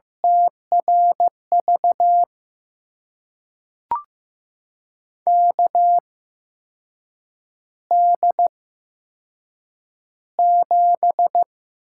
Morse code, heard as a keyed tone, about 15 wpm, spaced as for 6 wpm, 700 Hz. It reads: TRV KD7
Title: USA Call Signs - 6 Characters - 8x Character Spacing - 15wpm